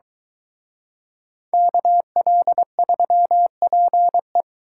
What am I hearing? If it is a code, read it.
Morse code, heard as a keyed tone, about 23 wpm, 700 Hz. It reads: KL3PE